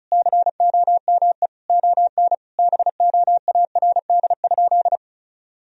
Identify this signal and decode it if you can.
Morse code, heard as a keyed tone, about 34 wpm, 700 Hz. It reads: COME ON BOARD?
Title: Sentences from Top 1000 Words - Review Part 4 - 35wpm